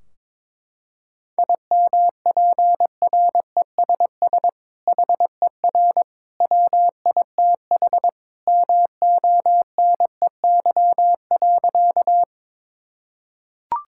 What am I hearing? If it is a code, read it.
Morse code, heard as a keyed tone, about 22 wpm, 700 Hz. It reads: IMPRESS HER WITH MONEY.
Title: Sentences from Top 1600 Words 22wpm